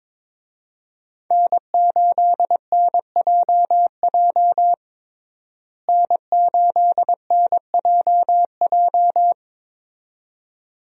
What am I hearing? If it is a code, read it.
Morse code, heard as a keyed tone, about 22 wpm, 700 Hz. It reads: N8NJJ N8NJJ
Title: USA Call Signs - 5 Characters - Repeated 2x - 22wpm